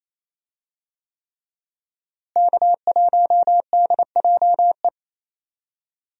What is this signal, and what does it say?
Morse code, heard as a keyed tone, about 28 wpm, 700 Hz. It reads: K1DJE
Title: USA Call Signs - 5 Characters - Repeated 3x - 28wpm